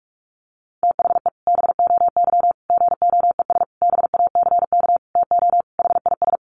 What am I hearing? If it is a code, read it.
Morse code, heard as a keyed tone, about 44 wpm, 700 Hz. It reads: THE BOY GOES BACK TO HIS
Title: Additional Easy Phrases 45wpm